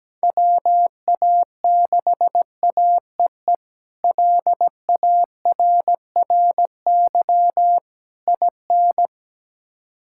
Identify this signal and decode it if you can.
Morse code, heard as a keyed tone, about 17 wpm, 700 Hz. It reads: WA6AEE LARRY IN
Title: K1USN SST QSO Practice - 17wpm